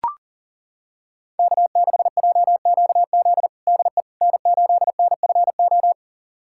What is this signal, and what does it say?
Morse code, heard as a keyed tone, about 40 wpm, 700 Hz. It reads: K6JQZ DE N8NFO
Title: USA Callsign DE Callsign - 5 Characters - 40wpm